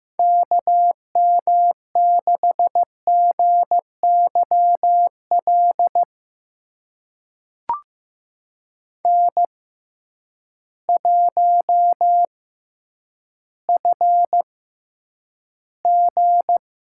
Morse code, heard as a keyed tone, about 15 wpm, spaced as for 9 wpm, 700 Hz. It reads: KM6GYL N1FG
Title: USA Call Signs - 6x Character Spacing - 15wpm